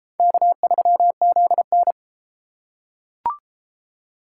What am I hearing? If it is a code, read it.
Morse code, heard as a keyed tone, about 32 wpm, 700 Hz. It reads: K3ZN